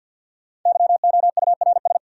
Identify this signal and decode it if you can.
Morse code, heard as a keyed tone, about 51 wpm, 700 Hz. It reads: YOURS
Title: Sentences from Top 1500 Words - New Word Drill Alt - 50wpm